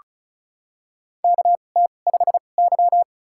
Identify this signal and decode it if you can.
Morse code, heard as a keyed tone, about 36 wpm, spaced as for 25 wpm, 700 Hz. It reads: KT5Y
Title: USA Call Signs - 4 Characters - 2x Character Spacing - 35wpm